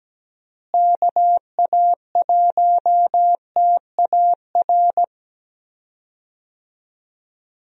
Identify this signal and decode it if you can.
Morse code, heard as a keyed tone, about 17 wpm, 700 Hz. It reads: KA1TAR
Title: USA Call Signs - 6 Characters - Repeated 2x - 17wpm